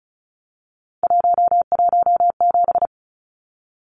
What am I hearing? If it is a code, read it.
Morse code, heard as a keyed tone, about 34 wpm, 700 Hz. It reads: 117
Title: Numbers - 3-Digit 35wpm